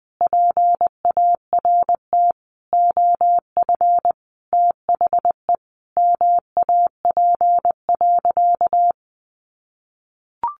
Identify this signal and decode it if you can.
Morse code, heard as a keyed tone, about 20 wpm, 700 Hz. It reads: PART OF THE MAP.
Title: Sentences from Top 400 Words 20wpm